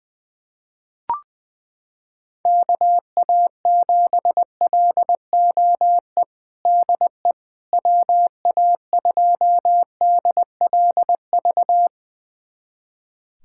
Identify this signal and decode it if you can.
Morse code, heard as a keyed tone, about 20 wpm, 700 Hz. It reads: KA7LOE DE WA2DLV
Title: USA Callsign DE Callsign - 6 Characters - 20wpm